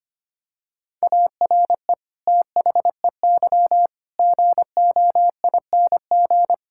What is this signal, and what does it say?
Morse code, heard as a keyed tone, about 25 wpm, 700 Hz. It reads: ARE THEY GOING
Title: Sentences from Top 1600 Words 25wpm